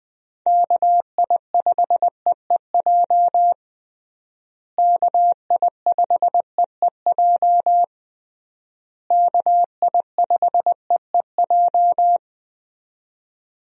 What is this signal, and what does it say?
Morse code, heard as a keyed tone, about 20 wpm, 700 Hz. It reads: KI5EEJ KI5EEJ KI5EEJ
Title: USA Call Signs - 6 Characters - Repeated 3x - 20wpm